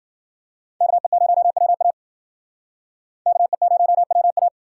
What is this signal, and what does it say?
Morse code, heard as a keyed tone, about 52 wpm, 700 Hz. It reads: KE0WA KE0WA